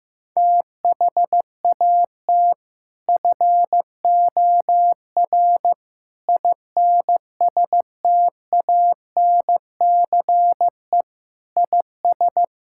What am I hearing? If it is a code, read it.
Morse code, heard as a keyed tone, about 15 wpm, 700 Hz. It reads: THAT FOR INSTANCE IS
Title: Sentences from Top 1000 Words - Review Part 5 - 15wpm